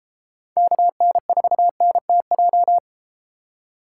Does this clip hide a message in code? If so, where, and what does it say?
Yes: Morse code from 0.6 to 2.8 s, heard as a keyed tone, about 32 wpm, 700 Hz. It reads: KN4NTJ